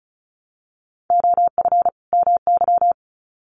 Morse code, heard as a keyed tone, about 34 wpm, 700 Hz. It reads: OF MY
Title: Sets of 2 Words 35wpm